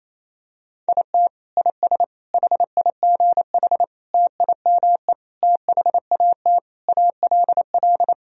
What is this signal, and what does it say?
Morse code, heard as a keyed tone, about 28 wpm, 700 Hz. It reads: IT IS HIGH TIME THAT ALL
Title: Sentences from Top 1000 Words - Review Part 1 - 28wpm